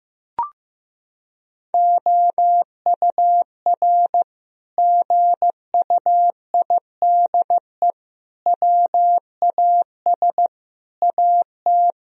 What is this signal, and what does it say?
Morse code, heard as a keyed tone, about 15 wpm, 700 Hz. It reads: OUR GUIDE WAS AT
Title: Sentences from Top 1000 Words - Review Part 3 - 15wpm